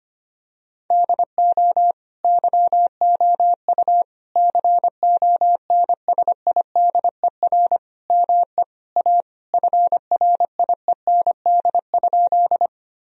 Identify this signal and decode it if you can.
Morse code, heard as a keyed tone, about 25 wpm, 700 Hz. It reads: DO YOU CONSIDER ME A FRIEND?
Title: Sentences from Top 1000 Words - Review Part 4 - 25wpm